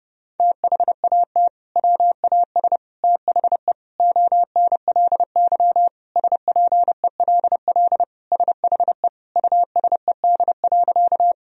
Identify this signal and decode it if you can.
Morse code, heard as a keyed tone, about 31 wpm, 700 Hz. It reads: THAT WAS THE ONLY SPELL SHE USED.
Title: Sentences from Top 1000 Words - Review Part 1 - 30wpm